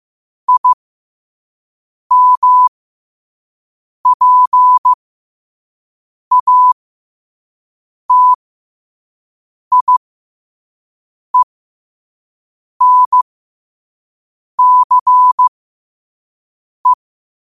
Morse code, heard as a keyed tone, about 15 wpm, spaced as for 5 wpm, 1000 Hz. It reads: IMPATIENCE
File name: qotdfiles / 5wpm